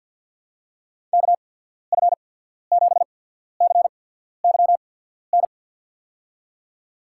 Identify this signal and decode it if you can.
Morse code, heard as a keyed tone, about 51 wpm, spaced as for 14 wpm, 700 Hz. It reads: KF7CYN
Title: USA Call Signs - 6 Characters - 8x Character Spacing - 50wpm